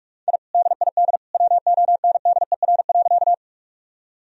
Morse code, heard as a keyed tone, about 41 wpm, 700 Hz. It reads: I DID WONDER.
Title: Sentences from Top 1000 Words - Review Part 3 - 45wpm